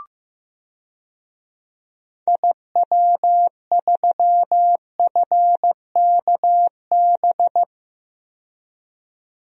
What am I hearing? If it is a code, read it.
Morse code, heard as a keyed tone, about 15 wpm, 700 Hz. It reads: IW3FKB
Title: DXCC-100 Call Signs - 6 Characters - 15wpm